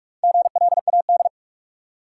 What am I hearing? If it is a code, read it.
Morse code, heard as a keyed tone, about 42 wpm, 700 Hz. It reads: GLAD
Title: Sentences from Top 800 Words - New Word Drill - 45wpm